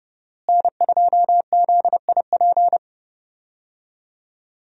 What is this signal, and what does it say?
Morse code, heard as a keyed tone, about 30 wpm, 700 Hz. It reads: N2ZIP
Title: USA Call Signs - 5 Characters - Repeated 3x - 30wpm